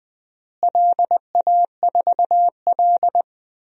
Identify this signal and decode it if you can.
Morse code, heard as a keyed tone, about 21 wpm, 700 Hz. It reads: LA4L
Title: DXCC-100 Call Signs - 4 Characters - Repeated 3x - 20wpm